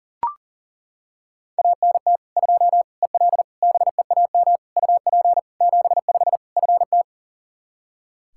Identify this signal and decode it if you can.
Morse code, heard as a keyed tone, about 40 wpm, 700 Hz. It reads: ANT 2 EL BEAM UP 75 FT
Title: Practice QSO - Exchange 2 - Info - 40wpm